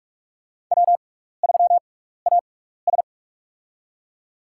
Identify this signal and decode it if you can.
Morse code, heard as a keyed tone, about 46 wpm, spaced as for 15 wpm, 700 Hz. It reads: W3AS
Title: USA Call Signs - 4 Characters - 6x Character Spacing - 45wpm